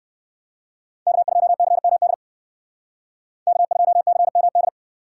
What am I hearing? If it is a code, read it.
Morse code, heard as a keyed tone, about 58 wpm, 700 Hz. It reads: K2CMD K2CMD